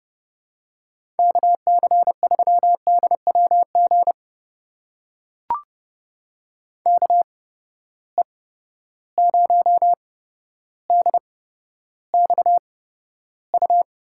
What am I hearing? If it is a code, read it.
Morse code, heard as a keyed tone, about 32 wpm, spaced as for 13 wpm, 700 Hz. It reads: KC3DWG KE0DXU
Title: USA Call Signs - 6 Characters - 8x Character Spacing - 30wpm